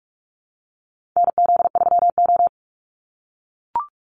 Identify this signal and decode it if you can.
Morse code, heard as a keyed tone, about 44 wpm, 700 Hz. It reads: NZ3O
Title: Call Signs - 45wpm